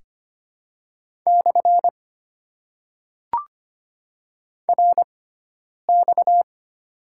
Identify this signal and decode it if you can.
Morse code, heard as a keyed tone, about 25 wpm, 700 Hz. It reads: / R X
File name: Top 90 QSO Elements - 6x Character Spacing - 25wpm